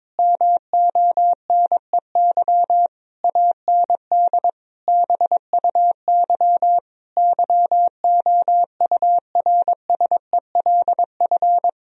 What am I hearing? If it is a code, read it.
Morse code, heard as a keyed tone, about 21 wpm, 700 Hz. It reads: MONEY AND BUY YOURSELF
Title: Sentences from Top 1000 Words - Review Part 4 - 22wpm